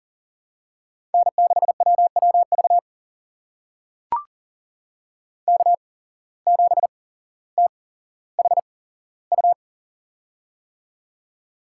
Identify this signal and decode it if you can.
Morse code, heard as a keyed tone, about 40 wpm, spaced as for 17 wpm, 700 Hz. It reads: N6WWV K7THU
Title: USA Call Signs - 5 Characters - 8x Character Spacing - 40wpm